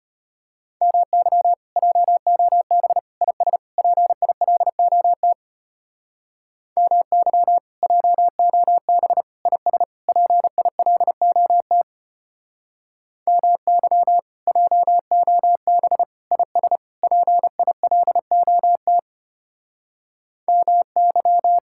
Morse code, heard as a keyed tone, about 33 wpm, 700 Hz. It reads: MY JOB IS PILOT MY JOB IS PILOT MY JOB IS PILOT MY